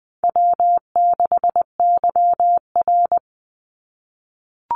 Morse code, heard as a keyed tone, about 20 wpm, 700 Hz. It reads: W6YR
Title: Easy Call Signs - 20wpm